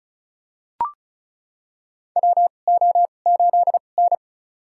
Morse code, heard as a keyed tone, about 35 wpm, spaced as for 25 wpm, 700 Hz. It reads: WO8N
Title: USA Call Signs - 4 Characters - 2x Character Spacing - 35wpm